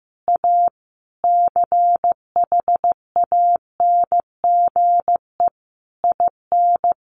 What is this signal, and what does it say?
Morse code, heard as a keyed tone, about 15 wpm, 700 Hz. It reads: A CHANGE IN
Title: A Sound Of Thunder 15wpm